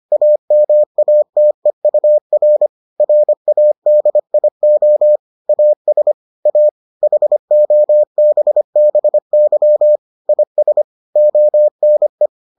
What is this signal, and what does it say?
Morse code, heard as a keyed tone, about 25 wpm, 600 Hz. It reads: AMATEUR RADIO AS A HOBBY IS ONE